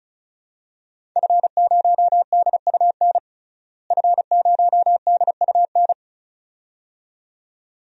Morse code, heard as a keyed tone, about 34 wpm, 700 Hz. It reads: F0DUN F0DUN
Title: DXCC-100 Call Signs - 5 Characters - Repeated 2x - 35wpm